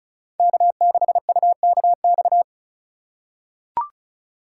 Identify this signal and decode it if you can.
Morse code, heard as a keyed tone, about 34 wpm, 700 Hz. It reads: K6UKX
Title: USA Call Signs - 5 Characters - 4x Character Spacing - 35wpm